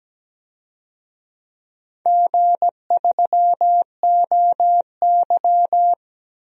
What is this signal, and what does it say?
Morse code, heard as a keyed tone, about 17 wpm, 700 Hz. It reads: G3OY
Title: DXCC-100 Call Signs - 4 Characters - 17wpm